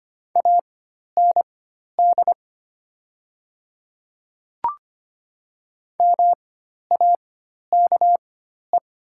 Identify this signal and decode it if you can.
Morse code, heard as a keyed tone, about 25 wpm, spaced as for 12 wpm, 690 Hz. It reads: AND MAKE
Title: 4x Character Spacing - Top 100 Words - 25wpm